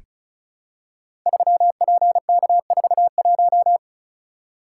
Morse code, heard as a keyed tone, about 36 wpm, 700 Hz. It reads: 3PK41
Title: Code Groups - 5 Letter:Numbers - 35wpm